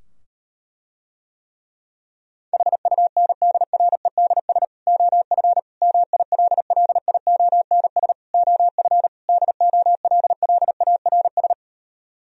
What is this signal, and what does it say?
Morse code, heard as a keyed tone, about 38 wpm, 700 Hz. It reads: HUNDREDS OF MILLIONS OF DOLLARS